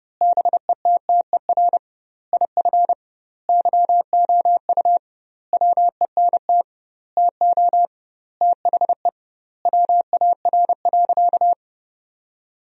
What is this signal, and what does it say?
Morse code, heard as a keyed tone, about 30 wpm, 700 Hz. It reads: BETTER IF YOU WENT TO THE WAR.
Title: 2x Word-Spacing - Sentences from Top 400 Words - 30wpm